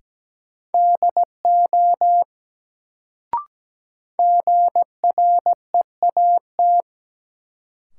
Morse code, heard as a keyed tone, about 17 wpm, 700 Hz. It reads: DO GREAT
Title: Sets of 2 Words - New Word Drill - 17wpm